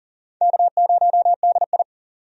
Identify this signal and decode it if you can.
Morse code, heard as a keyed tone, about 40 wpm, 700 Hz. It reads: K0DI
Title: USA Call Signs - 4 Characters - 40wpm